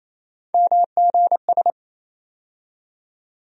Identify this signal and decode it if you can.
Morse code, heard as a keyed tone, about 28 wpm, 700 Hz. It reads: MGS
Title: USA Call Sign Postfixes - 3 Letters 28wpm